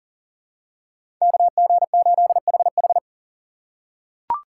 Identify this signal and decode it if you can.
Morse code, heard as a keyed tone, about 37 wpm, 700 Hz. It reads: KG8HH
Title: USA Call Signs - 5 Characters - 2x Character Spacing - 40wpm